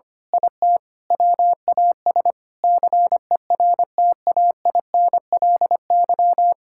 Morse code, heard as a keyed tone, about 25 wpm, 700 Hz. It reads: IT WAS CERTAINLY
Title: Sentences from Top 1600 Words 25wpm